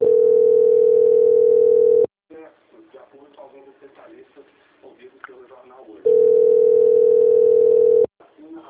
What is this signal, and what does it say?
Ringback tone (North America), 440 + 480 Hz, with speech.
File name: out- - - - - .27820